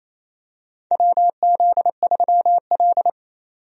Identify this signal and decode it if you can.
Morse code, heard as a keyed tone, about 28 wpm, 700 Hz. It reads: WZ3L
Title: USA Call Signs - 4 Characters - 28wpm